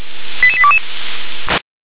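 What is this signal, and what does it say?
Five-tone selective call (ZVEI), heard as quick runs of five tones, tones of about 70 ms. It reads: 88822
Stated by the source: ZVEI five-tone selective call received on an SDR.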